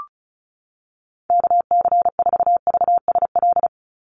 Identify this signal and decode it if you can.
Morse code, heard as a keyed tone, about 36 wpm, 700 Hz. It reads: KC4VSL
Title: USA Call Signs 35wpm